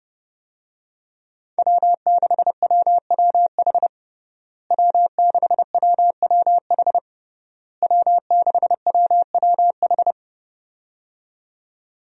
Morse code, heard as a keyed tone, about 32 wpm, 700 Hz. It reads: W6WWH W6WWH W6WWH
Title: USA Call Signs - 5 Characters - Repeated 3x - 30wpm